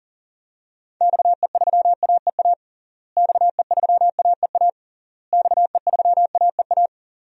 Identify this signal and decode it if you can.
Morse code, heard as a keyed tone, about 37 wpm, 700 Hz. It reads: XE3AEA XE3AEA XE3AEA
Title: DXCC-100 Call Signs - 6 Characters - Repeated 3x - 40wpm